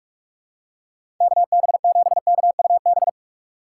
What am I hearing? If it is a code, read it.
Morse code, heard as a keyed tone, about 46 wpm, 700 Hz. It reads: KB7KUB
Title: USA Call Signs - 6 Characters - 2x Character Spacing - 45wpm